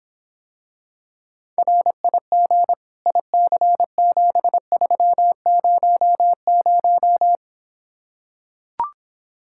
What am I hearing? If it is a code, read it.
Morse code, heard as a keyed tone, about 26 wpm, 700 Hz. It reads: RIG IC7300